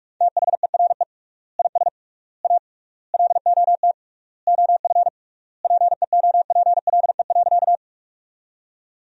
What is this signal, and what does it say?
Morse code, heard as a keyed tone, about 46 wpm, 700 Hz. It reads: THERE IS A LOT OF PEOPLE.
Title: 3x Word-Spacing - Sentences from Top 500 Words - 45wpm